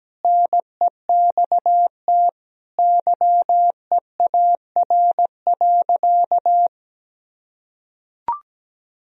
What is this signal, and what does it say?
Morse code, heard as a keyed tone, about 17 wpm, 700 Hz. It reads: NEXT YEAR.
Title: Sentences from Top 1000 Words - Review Part 3 - 17wpm